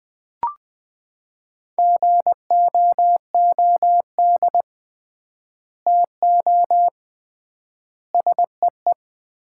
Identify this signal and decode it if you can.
Morse code, heard as a keyed tone, about 20 wpm, 700 Hz. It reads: GOOD TO SEE
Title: 3x Word-Spacing - Sets of 4 Words - 20wpm